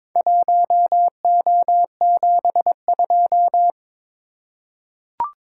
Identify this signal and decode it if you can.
Morse code, heard as a keyed tone, about 21 wpm, 700 Hz. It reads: 1O72
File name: practice / 22wpm / Four Letter-Number Combinations - 22wpm